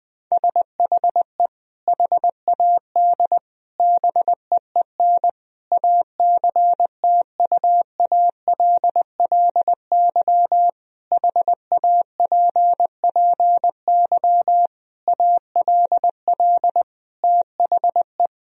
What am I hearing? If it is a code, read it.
Morse code, heard as a keyed tone, about 20 wpm, 700 Hz. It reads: SHE HAD BEEN ACTUALLY HAPPY ALL THE
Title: Sentences from Top 1000 Words - Review Part 4 - 20wpm